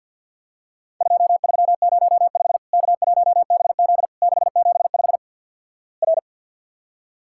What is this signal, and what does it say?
Morse code, heard as a keyed tone, about 48 wpm, 700 Hz. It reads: 2305 K1BZ 675 R
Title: Worked All Europe DX Contest - QTC Exchange - 50wpm